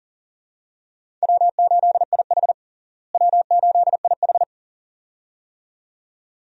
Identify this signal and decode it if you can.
Morse code, heard as a keyed tone, about 40 wpm, 700 Hz. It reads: W8IH W8IH
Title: USA Call Signs - 4 Characters - Repeated 2x - 40wpm